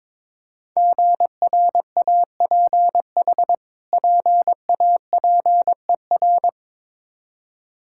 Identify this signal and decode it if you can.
Morse code, heard as a keyed tone, about 22 wpm, 700 Hz. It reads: GRAPH PAPER
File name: Difficult Words with Hints - 22wpm